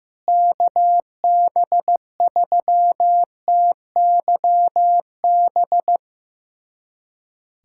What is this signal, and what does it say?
Morse code, heard as a keyed tone, about 15 wpm, 700 Hz. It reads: KB3TYB
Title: USA Callsign DE Callsign 15wpm